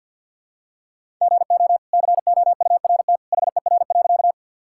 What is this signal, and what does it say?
Morse code, heard as a keyed tone, about 50 wpm, 700 Hz. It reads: GO COURT HER.
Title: Sentences from Top 1600 Words 50wpm